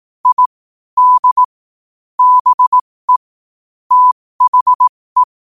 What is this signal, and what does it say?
Morse code, heard as a keyed tone, about 18 wpm, spaced as for 15 wpm, 1000 Hz. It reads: I D BE THE
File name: qotdfiles / 15wpm